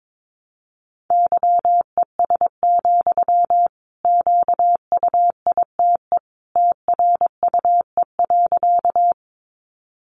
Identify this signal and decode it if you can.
Morse code, heard as a keyed tone, about 22 wpm, 700 Hz. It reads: YES, QUITE TRUE.
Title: Sentences from Top 700 Words 22wpm